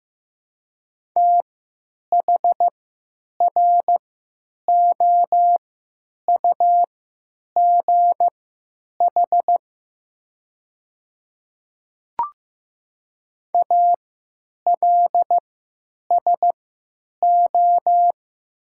Morse code, heard as a keyed tone, about 15 wpm, spaced as for 9 wpm, 700 Hz. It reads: THROUGH ALSO
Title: 3x Character Spacing - Top 500 Words - 15wpm